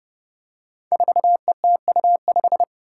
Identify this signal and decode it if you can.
Morse code, heard as a keyed tone, about 30 wpm, 700 Hz. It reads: 4ETU5
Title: Five Challenging Letter-Number Combinations 30wpm